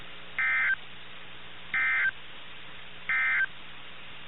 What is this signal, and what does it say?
Emergency Alert System (SAME) tones, heard as harsh buzzing bursts. It is heard from 0.3 to 3.5 s.
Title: NOAA Weather Radio - KIH61